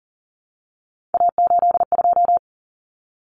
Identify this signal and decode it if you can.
Morse code, heard as a keyed tone, about 40 wpm, 700 Hz. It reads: A82
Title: Common Cut Numbers - 3-Digit 40wpm